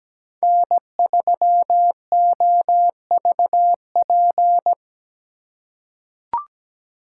Morse code, heard as a keyed tone, about 17 wpm, 700 Hz. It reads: N3OVP
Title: USA Call Signs - 5 Characters - 4x Character Spacing - 17wpm